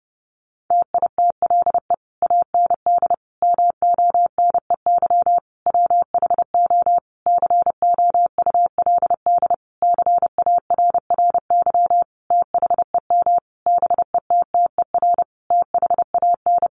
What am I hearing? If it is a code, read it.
Morse code, heard as a keyed tone, about 30 wpm, 700 Hz. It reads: TITLE AND MONEY WHO COULD CARRY THEM BETTER THAN
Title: Sentences from Top 1200 Words 30wpm